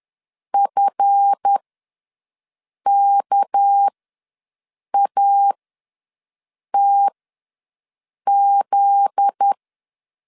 Morse code, heard as a keyed tone, about 10 wpm, spaced as for 5 wpm, 800 Hz. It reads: FKATZ
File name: LEZIONI / 9 2 ws 10wpm 4130